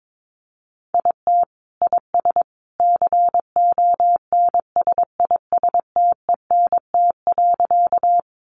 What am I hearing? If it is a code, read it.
Morse code, heard as a keyed tone, about 22 wpm, 700 Hz. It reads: IT IS CONSISTENT.
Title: Sentences from Top 1500 Words 22wpm